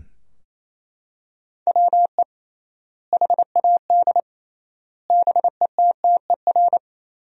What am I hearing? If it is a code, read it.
Morse code, heard as a keyed tone, about 28 wpm, 700 Hz. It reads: WE HAD BETTER
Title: 3x Word-Spacing - Sentences from Top 400 Words - 28wpm